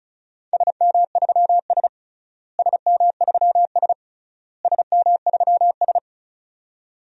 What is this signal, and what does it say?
Morse code, heard as a keyed tone, about 34 wpm, 700 Hz. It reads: SM3S SM3S SM3S
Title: DXCC-100 Call Signs - 4 Characters - Repeated 3x - 35wpm